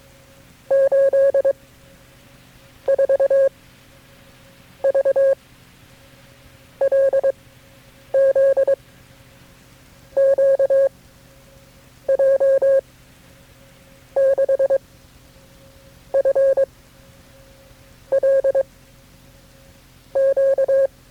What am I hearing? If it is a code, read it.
Morse code, heard as a keyed tone, about 22 wpm, 540 Hz. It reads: 8 4 V L Z Q J 6 F L Q
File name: FreeFiles / CW-practice / KB6MT-tape3-side1